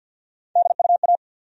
Morse code, heard as a keyed tone, about 50 wpm, 700 Hz. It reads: DUA
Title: USA Call Sign Postfixes - 3 Letters 50wpm